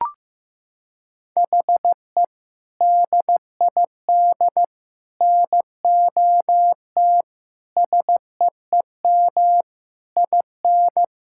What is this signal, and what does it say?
Morse code, heard as a keyed tone, about 15 wpm, 700 Hz. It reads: HE DID NOT SEEM IN
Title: Sentences from Top 1600 Words 15wpm